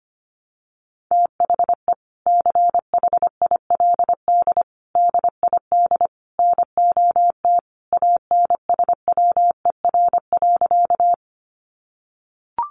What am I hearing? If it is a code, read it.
Morse code, heard as a keyed tone, about 25 wpm, 700 Hz. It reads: THE CHILD DID NOT ANSWER.
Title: Sentences from Top 800 Words 25wpm